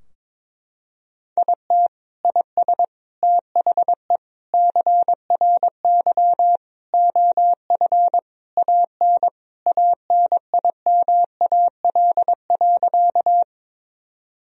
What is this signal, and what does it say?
Morse code, heard as a keyed tone, about 21 wpm, 700 Hz. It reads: IT IS THE CRY OF AN ANIMAL.
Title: Sentences from Top 1000 Words - Review Part 2 - 22wpm